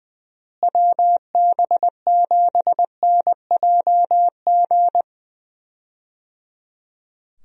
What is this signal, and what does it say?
Morse code, heard as a keyed tone, about 20 wpm, 700 Hz. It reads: WB7NJG
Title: USA Call Signs - 6 Characters - Repeated 3x - 20wpm